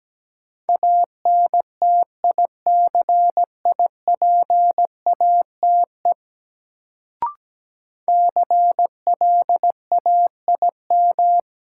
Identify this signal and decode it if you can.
Morse code, heard as a keyed tone, about 17 wpm, 700 Hz. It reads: ANTICIPATE CLAIM
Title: Sentences from Top 1500 Words - New Word Drill Alt - 17wpm